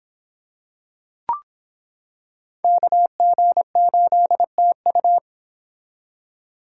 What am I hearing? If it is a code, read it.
Morse code, heard as a keyed tone, about 26 wpm, 700 Hz. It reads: KG8TU